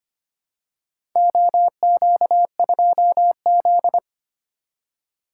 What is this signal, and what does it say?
Morse code, heard as a keyed tone, about 25 wpm, 700 Hz. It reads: OQ2Z
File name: practice / 25wpm / Four Letter-Number Combinations - 25wpm